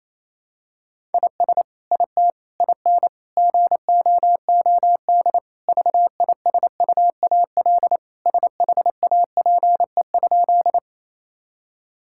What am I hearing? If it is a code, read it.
Morse code, heard as a keyed tone, about 28 wpm, 700 Hz. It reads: IS IT IN GOOD VISUAL SHAPE?
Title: Sentences from Top 1000 Words - Review Part 6 - 28wpm